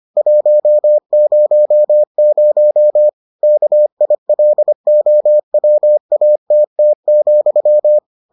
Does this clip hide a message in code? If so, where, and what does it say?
Yes: Morse code from 0.2 to 8.0 s, heard as a keyed tone, about 25 wpm, 600 Hz. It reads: 100 KILOWATT,